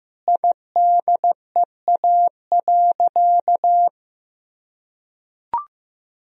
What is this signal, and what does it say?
Morse code, heard as a keyed tone, about 15 wpm, 700 Hz. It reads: IDEA.
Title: Sentences from Top 1000 Words - Review Part 4 - 15wpm